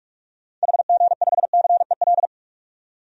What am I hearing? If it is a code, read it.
Morse code, heard as a keyed tone, about 45 wpm, 700 Hz. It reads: HG5CEL
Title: DXCC-100 Call Signs - 6 Characters - 45wpm